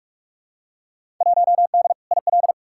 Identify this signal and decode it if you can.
Morse code, heard as a keyed tone, about 45 wpm, 700 Hz. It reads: 1D IL
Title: Field Day Exchanges - 45wpm